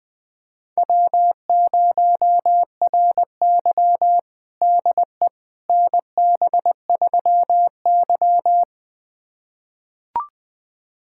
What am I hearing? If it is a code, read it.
Morse code, heard as a keyed tone, about 20 wpm, 700 Hz. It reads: W0RY DE NB3Y
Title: USA Callsign DE Callsign - 4 Characters - 20wpm